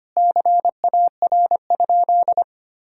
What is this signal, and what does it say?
Morse code, heard as a keyed tone, about 25 wpm, 700 Hz. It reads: CAR?
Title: Sentences from Top 1000 Words - Review Part 1 - 25wpm